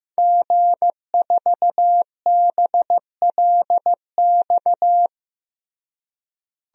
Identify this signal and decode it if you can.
Morse code, heard as a keyed tone, about 15 wpm, 700 Hz. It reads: G4BLX
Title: DXCC-100 Call Signs - 5 Characters - Repeated 2x - 15wpm